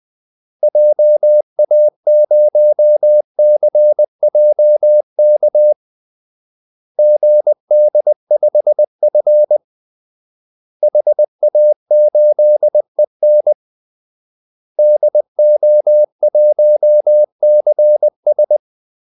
Morse code, heard as a keyed tone, about 20 wpm, 600 Hz. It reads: JA0CJK GD5F HA8EN DO1CS
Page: Kaldesignaler 20 wpm